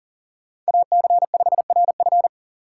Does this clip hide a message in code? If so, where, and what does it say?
Yes: Morse code from 0.7 to 2.3 s, heard as a keyed tone, about 52 wpm, 700 Hz. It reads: AC5RF